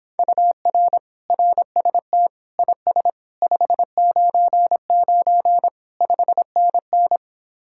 Morse code, heard as a keyed tone, about 26 wpm, 700 Hz. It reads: UR RST IS 599 5NN